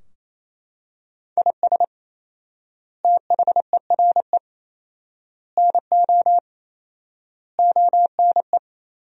Morse code, heard as a keyed tone, about 28 wpm, 700 Hz. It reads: IS THERE NO ONE
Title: 4x Word-Spacing - Sentences from Top 200 Words - 28wpm